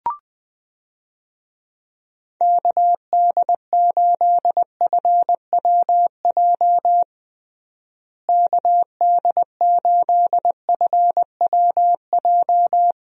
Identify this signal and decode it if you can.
Morse code, heard as a keyed tone, about 21 wpm, 700 Hz. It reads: KD8FWJ KD8FWJ
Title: USA Call Signs - 6 Characters - Repeated 3x - 20wpm